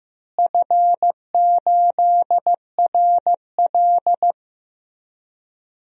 Morse code, heard as a keyed tone, about 15 wpm, 700 Hz. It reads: F8RL
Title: DXCC-100 Call Signs - 4 Characters - Repeated 2x - 15wpm